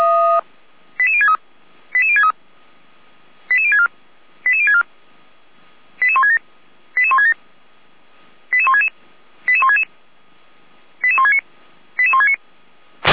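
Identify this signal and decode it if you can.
Five-tone selective call (ZVEI), heard as quick runs of five tones, tones of about 70 ms. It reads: 88063 88063 88064 88064 88167 88167 88166 88166 88169 88169
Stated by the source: ZVEI five-tone selective call received on an SDR.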